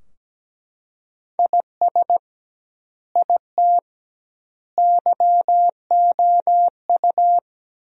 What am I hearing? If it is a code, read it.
Morse code, heard as a keyed tone, about 17 wpm, 700 Hz. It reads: IS IT YOU
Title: 2x Word-Spacing - Sentences from Top 200 Words - 17wpm